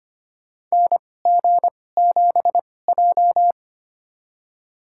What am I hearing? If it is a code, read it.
Morse code, heard as a keyed tone, about 25 wpm, spaced as for 18 wpm, 700 Hz. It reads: NG7J
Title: USA Call Signs - 2x Character Spacing - 25wpm